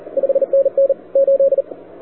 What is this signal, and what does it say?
Morse code, heard as a keyed tone, about 41 wpm, 530 Hz, with radio noise. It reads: 5NN 8
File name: wwcw08 / k5zd wwcw08 1700z